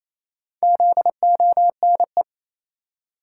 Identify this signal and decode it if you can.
Morse code, heard as a keyed tone, about 28 wpm, 700 Hz. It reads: ZONE
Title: Four Letter Words - 28wpm